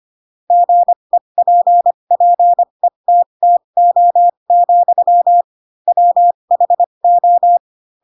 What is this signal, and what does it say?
Morse code, heard as a keyed tone, about 25 wpm, spaced as for 22 wpm, 700 Hz. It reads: GEPPETTO, WHO
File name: Pinocchio / Pinocchio ch7 18 25 ik1hge